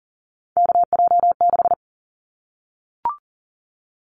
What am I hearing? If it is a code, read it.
Morse code, heard as a keyed tone, about 40 wpm, 700 Hz. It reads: KJ6
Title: Callsign Prefixes for Top 25 DXCC - Repeated 3x - 40wpm